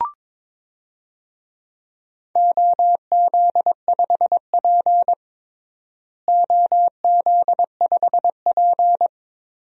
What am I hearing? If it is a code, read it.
Morse code, heard as a keyed tone, about 22 wpm, 700 Hz. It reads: OZ5P OZ5P
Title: DXCC-100 Call Signs - 4 Characters - Repeated 3x - 22wpm